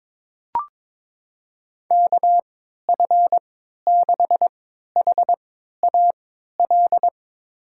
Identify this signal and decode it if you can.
Morse code, heard as a keyed tone, about 22 wpm, spaced as for 13 wpm, 700 Hz. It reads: KF6HAL
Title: USA Call Signs - 6 Characters - 3x Character Spacing - 22wpm